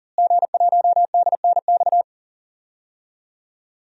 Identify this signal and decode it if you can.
Morse code, heard as a keyed tone, about 38 wpm, 700 Hz. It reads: G1DNX
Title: DXCC-100 Call Signs - 5 Characters - Repeated 3x - 40wpm